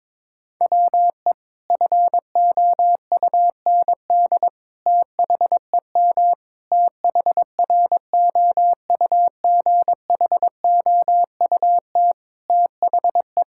Morse code, heard as a keyed tone, about 22 wpm, 700 Hz. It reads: WE FOUND THEM THROUGHOUT THE
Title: Sentences from Top 1000 Words - Review Part 5 - 22wpm